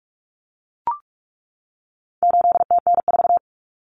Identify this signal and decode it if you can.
Morse code, heard as a keyed tone, about 44 wpm, 700 Hz. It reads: 8TN4
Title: Common Cut Numbers - 4-Digit 45wpm